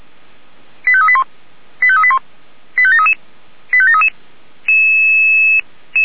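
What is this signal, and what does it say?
Five-tone selective call (ZVEI), heard as quick runs of five tones, tones of about 70 ms. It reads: 85381 85381 85733 85733
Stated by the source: ZVEI five-tone selective call received by SDR.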